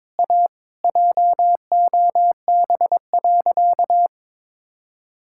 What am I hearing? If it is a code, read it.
Morse code, heard as a keyed tone, about 22 wpm, 700 Hz. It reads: A JOB.
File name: practice / 22wpm / Sentences with 3-Word Phrases - 22wpm